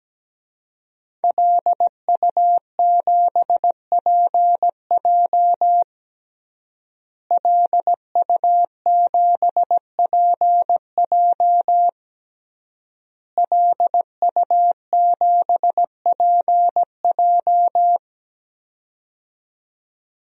Morse code, heard as a keyed tone, about 17 wpm, 700 Hz. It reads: LU7PJ LU7PJ LU7PJ
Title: DXCC-100 Call Signs - 5 Characters - Repeated 3x - 17wpm